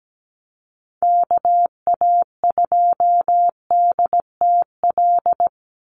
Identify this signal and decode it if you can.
Morse code, heard as a keyed tone, about 17 wpm, 700 Hz. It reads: KA2DTL
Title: USA Call Signs 17wpm